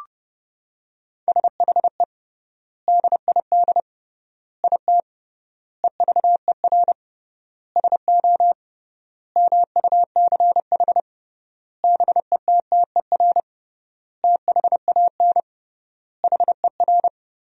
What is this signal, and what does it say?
Morse code, heard as a keyed tone, about 30 wpm, 700 Hz. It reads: SHE DID IT EVER SO MUCH BETTER THAN HER
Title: 3x Word-Spacing - Sentences from Top 400 Words - 30wpm